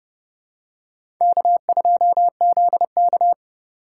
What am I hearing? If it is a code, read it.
Morse code, heard as a keyed tone, about 30 wpm, 700 Hz. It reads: K2ZK
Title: USA Call Signs - 4 Characters - 6x Character Spacing - 30wpm